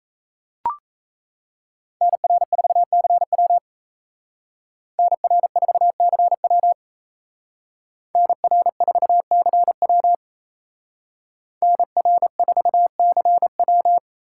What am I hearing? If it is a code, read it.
Morse code, heard as a keyed tone, about 32 wpm, 700 Hz. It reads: NR4CW NR4CW NR4CW NR4CW
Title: Speed Racing - USA 2x2 Call Signs 28wpm